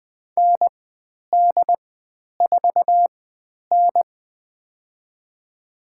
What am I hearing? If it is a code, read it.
Morse code, heard as a keyed tone, about 20 wpm, spaced as for 10 wpm, 700 Hz. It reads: ND4N
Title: 102 Lesson 17g - Letter u - Call Signs 20wpm